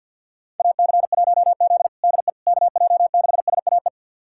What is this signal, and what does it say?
Morse code, heard as a keyed tone, about 49 wpm, 700 Hz. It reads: AC1Z DE KJ6SAE
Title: USA Callsign DE Callsign 50wpm